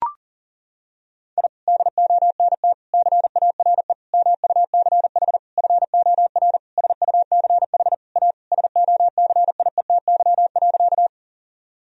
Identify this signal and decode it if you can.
Morse code, heard as a keyed tone, about 37 wpm, 700 Hz. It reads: I DONT CARE MUCH FOR SUCH A SOCIETY.
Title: Sentences from Top 1000 Words - Review Part 4 - 40wpm